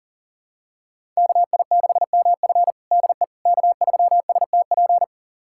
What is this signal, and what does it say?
Morse code, heard as a keyed tone, about 40 wpm, 700 Hz. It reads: KI6MF DE K3STP
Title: USA Callsign DE Callsign - 5 Characters - 40wpm